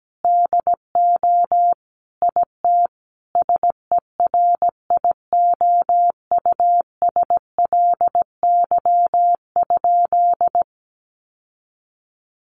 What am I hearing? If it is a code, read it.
Morse code, heard as a keyed tone, about 17 wpm, 700 Hz. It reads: DO IT SERIOUSLY?
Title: Sentences from Top 1200 Words 17wpm